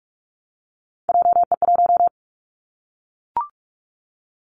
Morse code, heard as a keyed tone, about 44 wpm, 700 Hz. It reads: JE1
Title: Callsign Prefixes for Top 25 DXCC - Repeated 3x - 45wpm